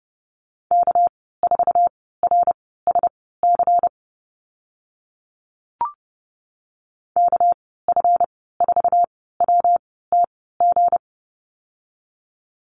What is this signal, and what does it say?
Morse code, heard as a keyed tone, about 30 wpm, spaced as for 17 wpm, 700 Hz. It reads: K4RSC KF4WTG
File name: Lesson 37g - Letter x - Call Signs 30wpm